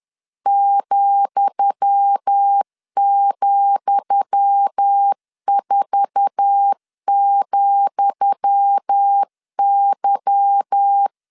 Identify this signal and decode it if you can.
Morse code, heard as a keyed tone, about 10 wpm, 800 Hz. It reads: ,,4,Y
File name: LEZIONI / 10 SIGNS / 10D COMA 10wpm 3013